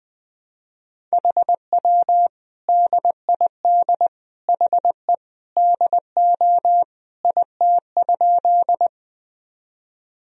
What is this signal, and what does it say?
Morse code, heard as a keyed tone, about 20 wpm, 700 Hz. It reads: HW DID HE DO IT?
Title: Sentences from Top 100 Words and Abbreviations 20wpm